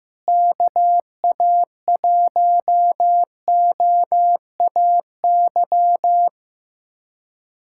Morse code, heard as a keyed tone, about 15 wpm, 700 Hz. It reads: KA1OAY
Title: USA Callsign DE Callsign 15wpm